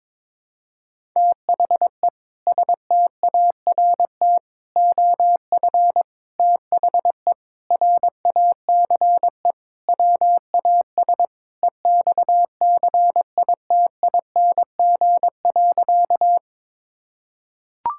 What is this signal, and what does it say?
Morse code, heard as a keyed tone, about 22 wpm, 700 Hz. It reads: THE START OF THE RACE WAS EXCITING.
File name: practice / 22wpm / Sentences with 4-Word Phrases - 22wpm